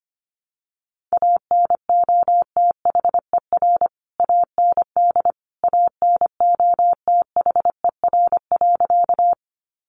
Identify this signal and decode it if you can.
Morse code, heard as a keyed tone, about 25 wpm, 700 Hz. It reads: ANOTHER AND ANOTHER.
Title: Sentences from Top 600 Words 25wpm